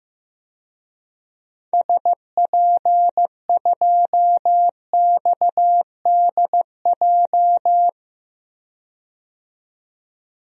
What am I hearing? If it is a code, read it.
Morse code, heard as a keyed tone, about 15 wpm, 700 Hz. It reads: SP2XDJ
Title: DXCC-100 Call Signs - 6 Characters - 15wpm